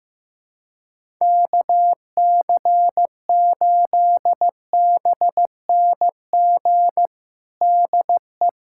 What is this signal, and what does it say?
Morse code, heard as a keyed tone, about 15 wpm, 700 Hz. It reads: KC8BNG DE
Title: USA Callsign DE Callsign - 6 Characters - 15wpm